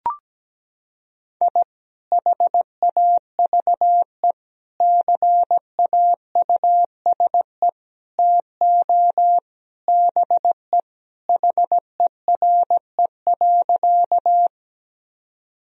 Morse code, heard as a keyed tone, about 17 wpm, 700 Hz. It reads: I HAVE CAUSE TO BE HERE.
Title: Sentences from Top 1000 Words - Review Part 1 - 17wpm